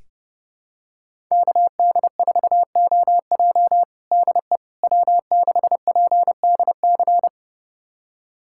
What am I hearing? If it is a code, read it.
Morse code, heard as a keyed tone, about 30 wpm, 700 Hz. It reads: KD4OJ DE W6PDC
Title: USA Callsign DE Callsign - 5 Characters - 30wpm